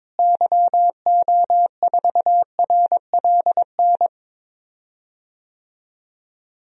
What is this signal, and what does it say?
Morse code, heard as a keyed tone, about 22 wpm, 700 Hz. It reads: YO4RLN
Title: DXCC-100 Call Signs - 6 Characters - Repeated 2x - 22wpm